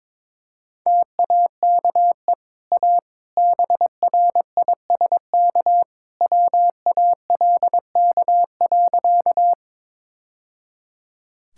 Morse code, heard as a keyed tone, about 22 wpm, 700 Hz. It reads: TAKE A BRISK WALK.
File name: practice / 22wpm / Easy Sentences with Hints - 22wpm